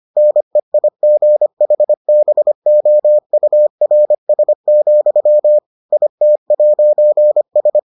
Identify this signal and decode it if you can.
Morse code, heard as a keyed tone, about 25 wpm, 600 Hz. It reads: NEIGHBOURS, IT'S